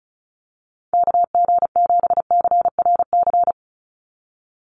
Morse code, heard as a keyed tone, about 36 wpm, 700 Hz. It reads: KG7CRC
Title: USA Call Signs 35wpm